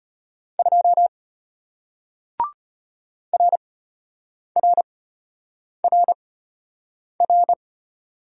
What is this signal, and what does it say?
Morse code, heard as a keyed tone, about 33 wpm, spaced as for 8 wpm, 700 Hz. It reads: 2 RRRR